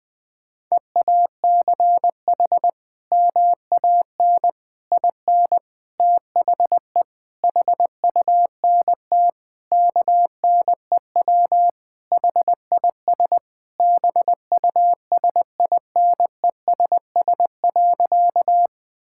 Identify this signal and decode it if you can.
Morse code, heard as a keyed tone, about 20 wpm, 700 Hz. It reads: EACH MAN IN THE HUNT KNEW HIS BUSINESS.
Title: Sentences from Top 1600 Words 20wpm